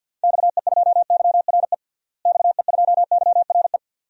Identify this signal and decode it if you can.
Morse code, heard as a keyed tone, about 49 wpm, 700 Hz. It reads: XE2YRE XE2YRE
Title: DXCC-100 Call Signs - 6 Characters - Repeated 2x - 50wpm